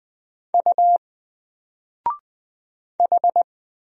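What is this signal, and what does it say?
Morse code, heard as a keyed tone, about 20 wpm, 700 Hz. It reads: U H
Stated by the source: Single letter and number Morse drill at 20 wpm, from N4LCD.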